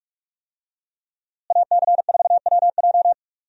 Morse code, heard as a keyed tone, about 47 wpm, 700 Hz. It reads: AC4WJ
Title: USA Call Signs - 5 Characters - Repeated 2x - 45wpm